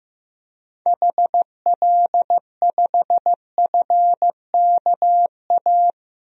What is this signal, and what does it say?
Morse code, heard as a keyed tone, about 15 wpm, 700 Hz. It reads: HL5FKA
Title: DXCC-100 Call Signs - 6 Characters - 15wpm